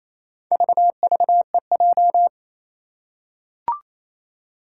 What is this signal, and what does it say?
Morse code, heard as a keyed tone, about 28 wpm, 700 Hz. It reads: VVEJ
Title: Four Challenging Letter-Number Combinations 28wpm